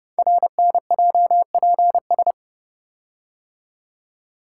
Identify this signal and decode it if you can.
Morse code, heard as a keyed tone, about 30 wpm, 700 Hz. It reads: RNJPS
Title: Code Groups - 5 Letters - 30wpm